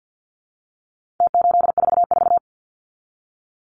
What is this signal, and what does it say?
Morse code, heard as a keyed tone, about 48 wpm, 700 Hz. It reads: T844
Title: Numbers - Contest Style - 4-Digit 50wpm